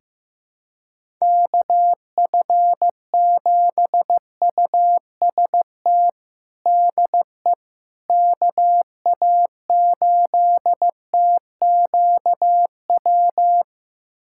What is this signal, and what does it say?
Morse code, heard as a keyed tone, about 15 wpm, 700 Hz. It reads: KF7UST DE KA8TQW
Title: USA Callsign DE Callsign 15wpm